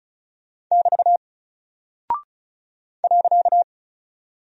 Morse code, heard as a keyed tone, about 36 wpm, 700 Hz. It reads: <BT> .
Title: Single Character (Letter:Number:Punctuation:Prosign) Alt 35wpm